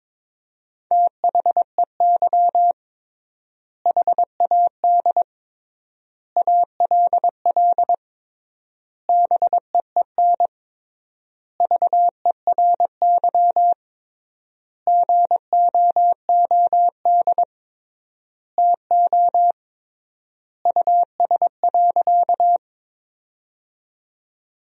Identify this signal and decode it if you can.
Morse code, heard as a keyed tone, about 22 wpm, 700 Hz. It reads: THEY HAD ALL BEEN VERY GOOD TO US.
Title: 3x Word-Spacing - Sentences from Top 200 Words - 22wpm